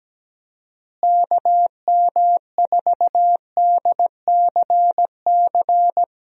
Morse code, heard as a keyed tone, about 17 wpm, 700 Hz. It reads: KM4DCC
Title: USA Call Signs - 6 Characters - Repeated 3x - 17wpm